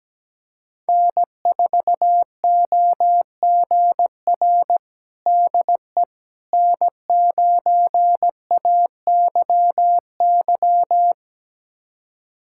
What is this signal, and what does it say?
Morse code, heard as a keyed tone, about 17 wpm, 700 Hz. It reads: N4OGR DE N9AYY
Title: USA Callsign DE Callsign - 5 Characters - 17wpm